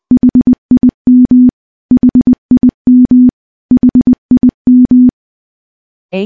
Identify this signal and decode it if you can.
Morse code, heard as a keyed tone, about 21 wpm, 250 Hz. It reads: HIM HIM HIM
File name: morse / morse-training-20WPM